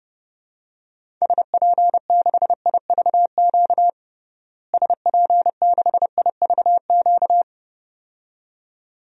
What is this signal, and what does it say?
Morse code, heard as a keyed tone, about 30 wpm, 700 Hz. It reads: SP6IVQ SP6IVQ
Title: DXCC-100 Call Signs - 6 Characters - Repeated 2x - 30wpm